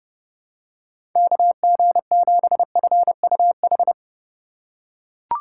Morse code, heard as a keyed tone, about 30 wpm, 700 Hz. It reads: KG7FUH
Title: USA Call Signs - 6 Characters - 4x Character Spacing - 30wpm